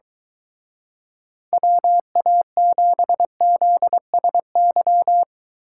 Morse code, heard as a keyed tone, about 23 wpm, 700 Hz. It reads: WA7ZSY